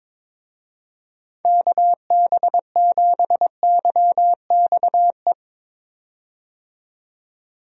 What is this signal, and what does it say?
Morse code, heard as a keyed tone, about 22 wpm, 700 Hz. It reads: KB7YXE
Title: USA Call Signs - 6 Characters - 22wpm